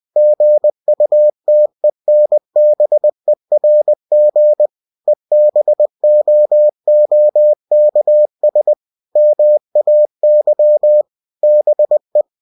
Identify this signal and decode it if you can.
Morse code, heard as a keyed tone, about 20 wpm, 600 Hz. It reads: GUTENBERG EBOOKS MAY BE
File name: cw-books / winnie-the-pooh / winnie-20wpm-page-0112